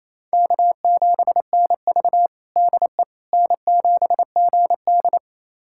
Morse code, heard as a keyed tone, about 28 wpm, 700 Hz. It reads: K7NV DE N7GD
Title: USA Callsign DE Callsign - 4 Characters - 28wpm